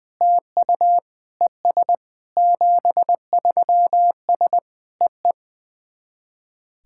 Morse code, heard as a keyed tone, about 20 wpm, 690 Hz. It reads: TU ES 73S EE
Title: SOTA QSO Practice 20wpm